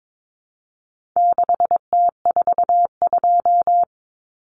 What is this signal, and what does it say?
Morse code, heard as a keyed tone, about 22 wpm, 700 Hz. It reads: 6T42
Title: Common Cut Numbers - 4-Digit 22wpm